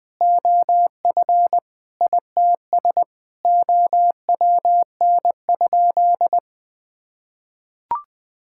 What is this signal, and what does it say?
Morse code, heard as a keyed tone, about 20 wpm, 700 Hz. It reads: OF ITS OWN?
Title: Sentences from Top 1000 Words - Review Part 5 - 20wpm